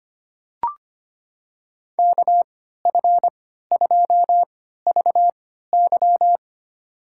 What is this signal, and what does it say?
Morse code, heard as a keyed tone, about 25 wpm, spaced as for 14 wpm, 700 Hz. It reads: KF2VY
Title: USA Call Signs - 3x Character Spacing - 25wpm